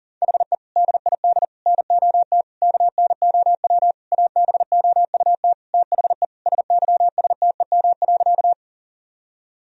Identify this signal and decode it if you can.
Morse code, heard as a keyed tone, about 40 wpm, 700 Hz. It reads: HE DID NOT KNOW ABOUT THE SYSTEM.
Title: Sentences from Top 1000 Words - Review Part 3 - 40wpm